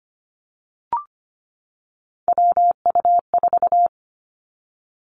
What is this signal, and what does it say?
Morse code, heard as a keyed tone, about 25 wpm, 700 Hz. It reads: WU4
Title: Three Challenging Letter-Number Combinations 25wpm